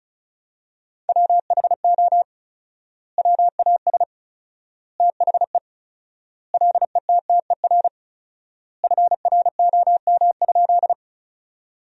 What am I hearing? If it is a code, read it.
Morse code, heard as a keyed tone, about 36 wpm, 700 Hz. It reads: WHO WAS THE LETTER FROM?
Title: 4x Word-Spacing - Sentences from Top 300 Words - 35wpm